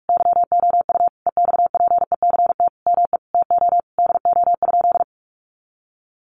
Morse code, heard as a keyed tone, about 44 wpm, 700 Hz. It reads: YOU EXPECT ME TO DO?
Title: Sentences from Top 1100 Words 45wpm